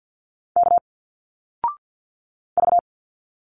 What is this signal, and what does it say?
Morse code, heard as a keyed tone, about 50 wpm, 700 Hz. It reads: K V
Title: Single Letters 50wpm